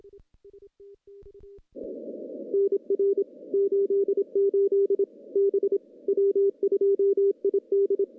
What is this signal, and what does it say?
Morse code, heard as a keyed tone, about 25 wpm, 400 Hz. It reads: NR 88 B W2ID
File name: contests / CW-SS-k5tr- -0800